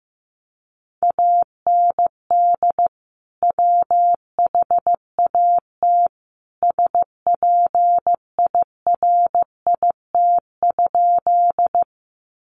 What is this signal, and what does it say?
Morse code, heard as a keyed tone, about 15 wpm, 700 Hz. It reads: AND WHAT SPIRIT?
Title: Sentences from Top 1400 Words 15wpm